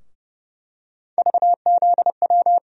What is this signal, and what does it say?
Morse code, heard as a keyed tone, about 30 wpm, 700 Hz. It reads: VZW
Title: USA Call Sign Postfixes - 3 Letters 30wpm